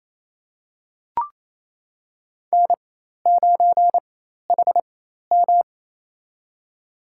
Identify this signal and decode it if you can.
Morse code, heard as a keyed tone, about 28 wpm, spaced as for 13 wpm, 700 Hz. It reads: N9HM
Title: USA Call Signs - 4 Characters - 4x Character Spacing - 28wpm